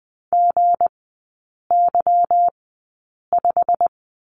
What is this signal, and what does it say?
Morse code, heard as a keyed tone, about 20 wpm, spaced as for 8 wpm, 690 Hz. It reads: GY5
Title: 172 Lesson 28e - Number 9 - Three Characters 20wpm